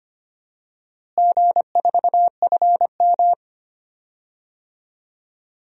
Morse code, heard as a keyed tone, about 25 wpm, 700 Hz. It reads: G4FM
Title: DXCC-100 Call Signs - 4 Characters - Repeated 2x - 25wpm